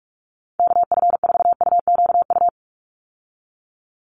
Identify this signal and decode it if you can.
Morse code, heard as a keyed tone, about 44 wpm, 700 Hz. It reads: KF4UQU
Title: USA Call Signs 45wpm